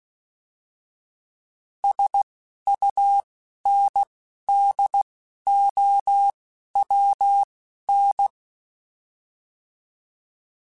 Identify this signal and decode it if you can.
Morse code, heard as a keyed tone, about 16 wpm, spaced as for 11 wpm, 800 Hz. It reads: SUNDOWN